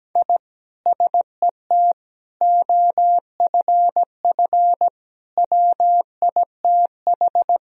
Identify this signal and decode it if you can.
Morse code, heard as a keyed tone, about 17 wpm, 700 Hz. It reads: I SET OFF WITH
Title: Sentences from Top 1000 Words - Review Part 1 - 17wpm